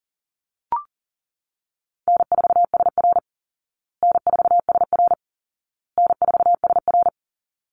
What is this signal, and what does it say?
Morse code, heard as a keyed tone, about 40 wpm, 700 Hz. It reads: N4SR N4SR N4SR
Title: Call Signs Repeated 2x - 40wpm